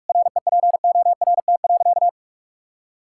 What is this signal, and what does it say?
Morse code, heard as a keyed tone, about 46 wpm, 690 Hz. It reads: REPORT.
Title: Sentences from Top 1000 Words - Review Part 6 - 45wpm